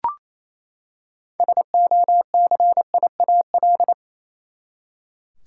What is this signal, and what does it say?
Morse code, heard as a keyed tone, about 28 wpm, 700 Hz. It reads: SOCIAL
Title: K1USN SST QSO Practice 28wpm